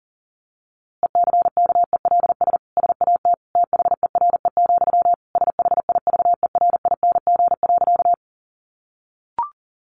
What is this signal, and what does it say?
Morse code, heard as a keyed tone, about 40 wpm, 700 Hz. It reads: ECKELS SAT THERE, SHIVERING.
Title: A Sound Of Thunder 40wpm